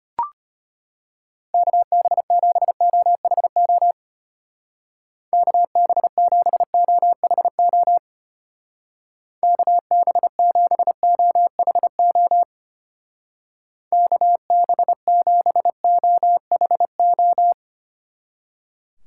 Morse code, heard as a keyed tone, about 29 wpm, 700 Hz. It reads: KB7OHO KB7OHO KB7OHO KB7OHO
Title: Speed Racing - USA 2x3 Call Signs 25wpm